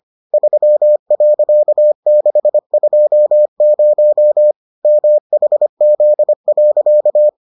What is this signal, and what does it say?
Morse code, heard as a keyed tone, about 25 wpm, 600 Hz. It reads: 3.620 MHZ.